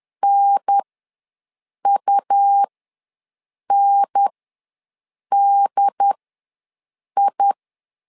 Morse code, heard as a keyed tone, about 10 wpm, spaced as for 6 wpm, 800 Hz. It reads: NUNDI
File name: LEZIONI / 2 UHDN / 2 UHDN ws 10wpm 2044